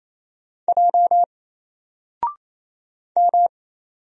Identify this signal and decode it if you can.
Morse code, heard as a keyed tone, about 27 wpm, 700 Hz. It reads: J M
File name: Single-Letter-Number-28wpm